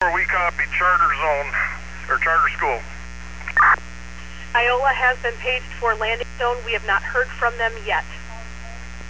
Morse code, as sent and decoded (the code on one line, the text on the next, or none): none